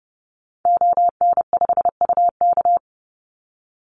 Morse code, heard as a keyed tone, about 30 wpm, 700 Hz. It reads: ON5UK
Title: Call Signs - 30wpm